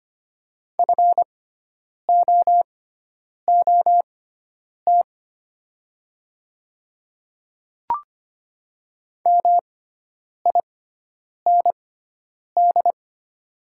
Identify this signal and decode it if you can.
Morse code, heard as a keyed tone, about 25 wpm, spaced as for 9 wpm, 700 Hz. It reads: FOOT MIND
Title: 6x Character Spacing - Top 500 Words - 25wpm